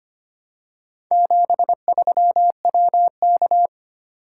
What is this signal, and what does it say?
Morse code, heard as a keyed tone, about 25 wpm, 700 Hz. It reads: 73WK
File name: practice / 25wpm / Four Letter-Number Combinations - 25wpm